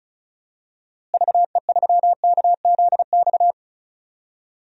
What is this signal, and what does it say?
Morse code, heard as a keyed tone, about 36 wpm, 700 Hz. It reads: VE3KZX
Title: DXCC-100 Call Signs - 6 Characters - Repeated 3x - 35wpm